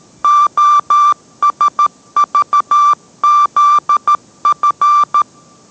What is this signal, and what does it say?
Morse code, heard as a keyed tone, about 13 wpm, 1180 Hz. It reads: OSVZF